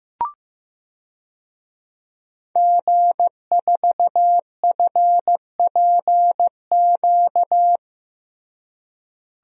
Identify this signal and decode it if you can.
Morse code, heard as a keyed tone, about 15 wpm, 700 Hz. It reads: G4FPQ
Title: DXCC-100 Call Signs - 5 Characters - Repeated 3x - 15wpm